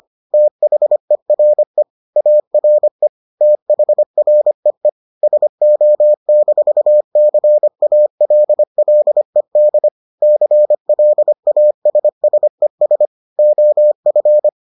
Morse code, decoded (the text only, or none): THERE ARE THREE SO-CALLED CLASSES OF